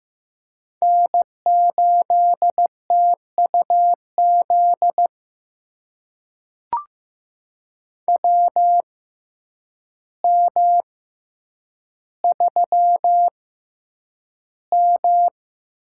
Morse code, heard as a keyed tone, about 15 wpm, spaced as for 8 wpm, 700 Hz. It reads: N8TUZ WM3M
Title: USA Call Signs - 6x Character Spacing - 15wpm